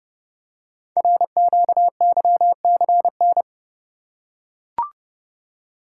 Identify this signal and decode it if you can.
Morse code, heard as a keyed tone, about 28 wpm, 700 Hz. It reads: RQYCN
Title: Code Groups - 5 Letters - 30wpm